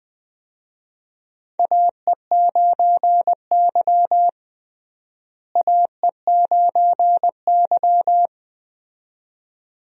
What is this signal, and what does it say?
Morse code, heard as a keyed tone, about 20 wpm, 700 Hz. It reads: AE9Y AE9Y
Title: USA Call Signs - 4 Characters - Repeated 2x - 20wpm